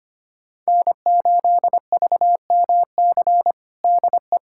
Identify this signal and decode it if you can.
Morse code, heard as a keyed tone, about 25 wpm, 700 Hz. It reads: N8VMC DE
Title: USA Callsign DE Callsign - 5 Characters - 25wpm